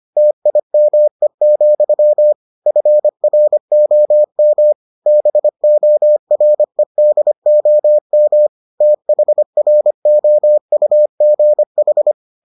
Morse code, heard as a keyed tone, about 25 wpm, 600 Hz. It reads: TIME, FROM BOREDOM THROUGH